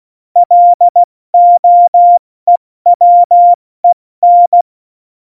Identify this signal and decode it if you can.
Morse code, heard as a keyed tone, about 16 wpm, 700 Hz. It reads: LOEWEN
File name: Morsekurs / Witze / Witz 09. Oktober 2018, 70 F80